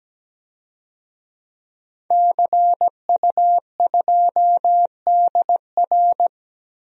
Morse code, heard as a keyed tone, about 17 wpm, 700 Hz. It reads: CU2DR
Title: DXCC-100 Call Signs - 5 Characters - 17wpm